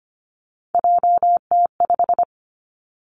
Morse code, heard as a keyed tone, about 25 wpm, 700 Hz. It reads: JT5
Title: Three Challenging Letter-Number Combinations 25wpm